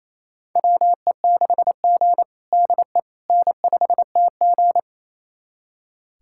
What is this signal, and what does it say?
Morse code, heard as a keyed tone, about 27 wpm, 700 Hz. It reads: WE6G DE N5TG
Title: USA Callsign DE Callsign - 4 Characters - 28wpm